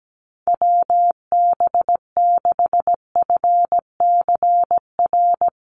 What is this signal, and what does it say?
Morse code, heard as a keyed tone, about 17 wpm, 700 Hz. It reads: WB6FCR
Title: USA Call Signs 17wpm